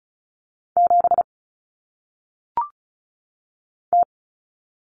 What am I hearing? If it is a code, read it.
Morse code, heard as a keyed tone, about 36 wpm, 700 Hz. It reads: 7 T
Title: Common Cut Numbers - 1-Digit 35wpm